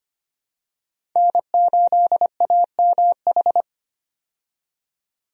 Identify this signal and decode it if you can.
Morse code, heard as a keyed tone, about 25 wpm, 700 Hz. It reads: N8AMH
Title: USA Call Signs - 5 Characters - 25wpm